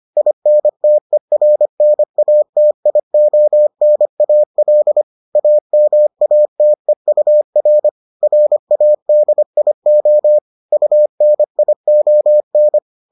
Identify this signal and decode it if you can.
Morse code, heard as a keyed tone, about 25 wpm, 600 Hz. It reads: INTERNATIONAL AMATEUR RADIO UNION